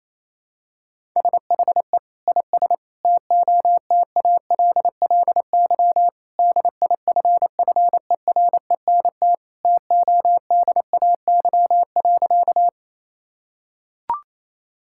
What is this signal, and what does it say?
Morse code, heard as a keyed tone, about 28 wpm, 700 Hz. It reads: SHE IS TOTALLY DIFFERENT TODAY.
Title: Sentences from Top 1600 Words 28wpm